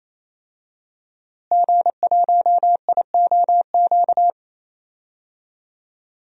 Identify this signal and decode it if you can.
Morse code, heard as a keyed tone, about 28 wpm, 700 Hz. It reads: G1IOQ
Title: DXCC-100 Call Signs - 5 Characters - Repeated 3x - 28wpm